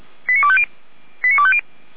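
Five-tone selective call (ZVEI), heard as quick runs of five tones, tones of about 70 ms. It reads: 89250 89250
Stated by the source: ZVEI five-tone selective call received on SDR.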